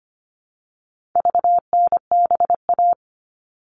Morse code, heard as a keyed tone, about 25 wpm, 700 Hz. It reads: VNBA
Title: All Cut Numbers - 4-Digit 25wpm